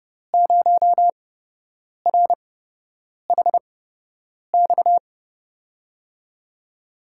Morse code, heard as a keyed tone, about 30 wpm, spaced as for 8 wpm, 700 Hz. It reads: 0RHX
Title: USA Call Signs - 5 Characters - 8x Character Spacing - 30wpm